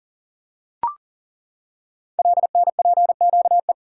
Morse code, heard as a keyed tone, about 39 wpm, 690 Hz. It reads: LNPQE
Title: Code Groups - 5 Letters - 40wpm